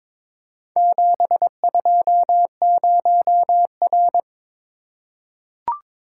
Morse code, heard as a keyed tone, about 23 wpm, 700 Hz. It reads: 720R
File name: practice / 22wpm / Four Letter-Number Combinations - 22wpm